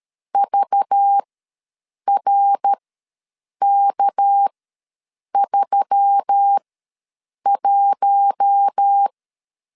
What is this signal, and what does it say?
Morse code, heard as a keyed tone, about 13 wpm, spaced as for 7 wpm, 800 Hz. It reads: VRK31
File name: LEZIONI / 6 PKX0 / 6 PK ws 12wpm 3116